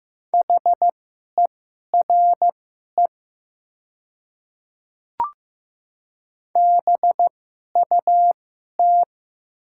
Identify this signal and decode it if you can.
Morse code, heard as a keyed tone, about 15 wpm, spaced as for 11 wpm, 700 Hz. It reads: HERE BUT
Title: 2x Character Spacing - Top 300 Words - 15wpm